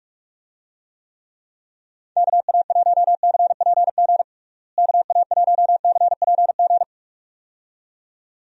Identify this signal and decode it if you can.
Morse code, heard as a keyed tone, about 46 wpm, 700 Hz. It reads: KA1CPG KA1CPG
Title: USA Call Signs - 6 Characters - Repeated 2x - 45wpm